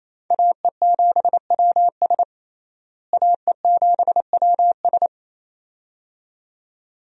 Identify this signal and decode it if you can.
Morse code, heard as a keyed tone, about 28 wpm, 700 Hz. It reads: AE7WS AE7WS
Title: USA Call Signs - 5 Characters - Repeated 2x - 28wpm